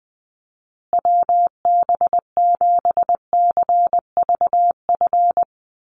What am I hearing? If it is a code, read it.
Morse code, heard as a keyed tone, about 20 wpm, 700 Hz. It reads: WB7CVF
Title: USA Call Signs Repeated 2x 20wpm